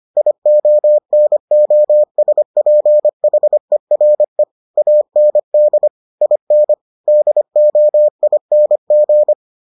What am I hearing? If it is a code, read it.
Morse code, heard as a keyed tone, about 25 wpm, 600 Hz. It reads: IONOSPHERE AND IN DOING